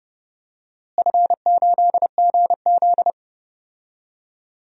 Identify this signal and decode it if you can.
Morse code, heard as a keyed tone, about 30 wpm, 700 Hz. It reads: F8GZ
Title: DXCC-100 Call Signs - 4 Characters - Repeated 2x - 30wpm